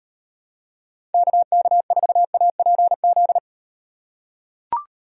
Morse code, heard as a keyed tone, about 38 wpm, 700 Hz. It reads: KK4APZ